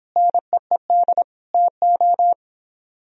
Morse code, heard as a keyed tone, about 31 wpm, 700 Hz. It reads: NEED TO